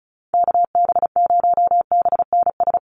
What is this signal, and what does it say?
Morse code, heard as a keyed tone, about 34 wpm, 700 Hz. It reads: KB0BNS
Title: USA Call Signs Repeated 2x 35wpm